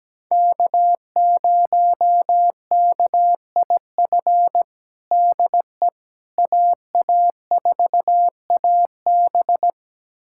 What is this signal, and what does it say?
Morse code, heard as a keyed tone, about 17 wpm, 700 Hz. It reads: K0KIF DE AA4AB
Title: USA Callsign DE Callsign - 5 Characters - 17wpm